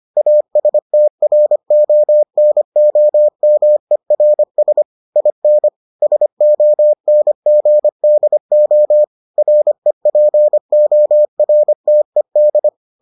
Morse code, heard as a keyed tone, about 25 wpm, 600 Hz. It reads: ASTRONOMERS IN SONGDO REPORTED